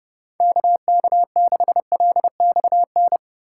Morse code, heard as a keyed tone, about 30 wpm, 700 Hz. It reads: KK6LXN